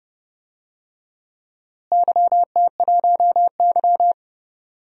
Morse code, heard as a keyed tone, about 32 wpm, 700 Hz. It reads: YT1Y
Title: DXCC-100 Call Signs - 4 Characters - Repeated 2x - 30wpm